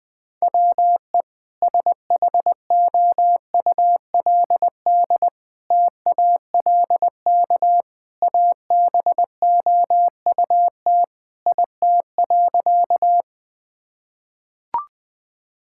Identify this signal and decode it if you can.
Morse code, heard as a keyed tone, about 21 wpm, 700 Hz. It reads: WE SHOULD TALK ABOUT IT.
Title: Sentences with 2-Word Phrases - 20wpm